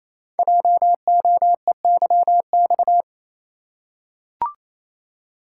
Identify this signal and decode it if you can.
Morse code, heard as a keyed tone, about 28 wpm, 700 Hz. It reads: JOEYX
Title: Code Groups - 5 Letter:Numbers - 28wpm